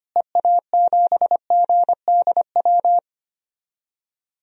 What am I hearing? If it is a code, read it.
Morse code, heard as a keyed tone, about 25 wpm, 700 Hz. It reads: EA7GDW
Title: DXCC-100 Call Signs - 6 Characters - Repeated 2x - 25wpm